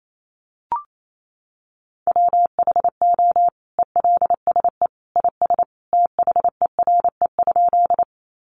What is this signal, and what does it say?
Morse code, heard as a keyed tone, about 28 wpm, 700 Hz. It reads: WHO ELSE IS THERE?
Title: Sentences from Top 800 Words 28wpm